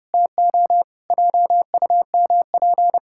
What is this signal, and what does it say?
Morse code, heard as a keyed tone, about 30 wpm, 700 Hz. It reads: TO JUMP
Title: Pangram Challenge v2 - 30wpm